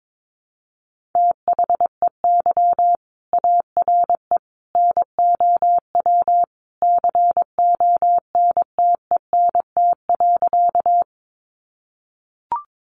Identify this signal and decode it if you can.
Morse code, heard as a keyed tone, about 22 wpm, 700 Hz. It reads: THEY ARE NOW CONTENT.
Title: Sentences from Top 900 Words 22wpm